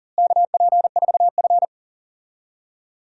Morse code, heard as a keyed tone, about 40 wpm, 700 Hz. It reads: KP4F
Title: DXCC-100 Call Signs - 4 Characters - Repeated 2x - 40wpm